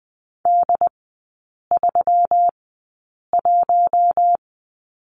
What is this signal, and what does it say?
Morse code, heard as a keyed tone, about 20 wpm, spaced as for 8 wpm, 700 Hz. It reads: D31
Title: 138 Lesson 23e - Letter f - Three Characters 20wpm